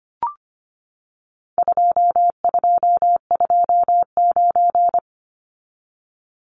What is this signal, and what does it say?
Morse code, heard as a keyed tone, about 25 wpm, 700 Hz. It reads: 2229
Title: Numbers - 4-Digit 25wpm